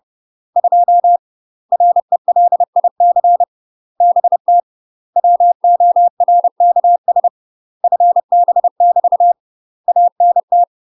Morse code, heard as a keyed tone, about 30 wpm, 700 Hz. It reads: 2 RELIC BT WORKS FB <BT> ANT
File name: UK MASTER QSOs TRAINING / 30WPM / UK MASTER QSOs 0015